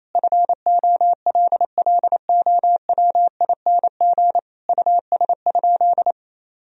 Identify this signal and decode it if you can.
Morse code, heard as a keyed tone, about 28 wpm, 700 Hz. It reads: FOLLOWING US?
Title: Sentences from Top 1000 Words - Review Part 4 - 28wpm